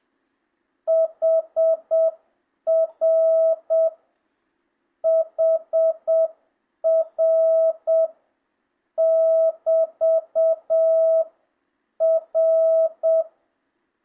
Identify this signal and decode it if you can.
Morse code, heard as a keyed tone, about 7 wpm, 660 Hz. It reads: HR HR <BT> R
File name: jh8cbh / 2025cw